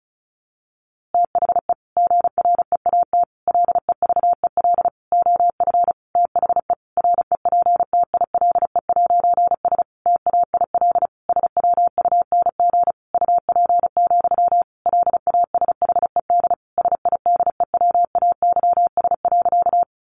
Morse code, heard as a keyed tone, about 34 wpm, 700 Hz. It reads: THE GREAT LEVEL OF THE REPTILE'S TAIL SWUNG UP, LASHED SIDEWAYS.
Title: A Sound Of Thunder 35wpm